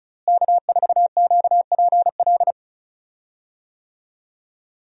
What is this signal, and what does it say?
Morse code, heard as a keyed tone, about 34 wpm, 700 Hz. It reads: K4QPL
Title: USA Call Signs - 5 Characters - Repeated 2x - 35wpm